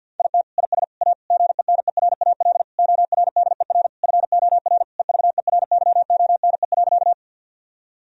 Morse code, heard as a keyed tone, about 52 wpm, 700 Hz. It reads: IT IS A GENERAL ORDER FOR EVERYONE.
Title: Sentences from Top 1000 Words - Review Part 3 - 50wpm